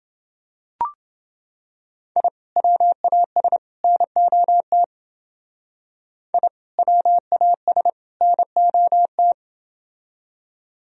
Morse code, heard as a keyed tone, about 28 wpm, 700 Hz. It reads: I WAS NOT I WAS NOT